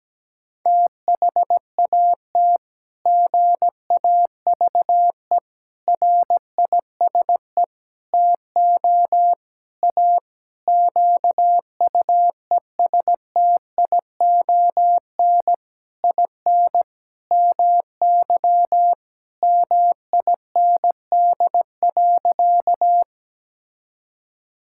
Morse code, heard as a keyed tone, about 17 wpm, 700 Hz. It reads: THAT GAVE RISE TO A QUESTION IN MY MIND.
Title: Sentences from Top 1000 Words - Review Part 6 - 17wpm